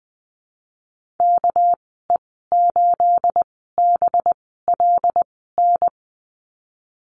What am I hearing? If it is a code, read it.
Morse code, heard as a keyed tone, about 20 wpm, spaced as for 14 wpm, 700 Hz. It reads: KE8BLN
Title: 231 Lesson 37g - Letter x - Call Signs 20wpm